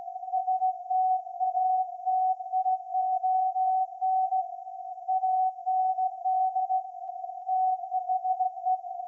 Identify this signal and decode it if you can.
Morse code, heard as a keyed tone, about 16 wpm, 750 Hz. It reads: STATION AND THE
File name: iz7slz / VLF / SAQ 30-06-2024 0900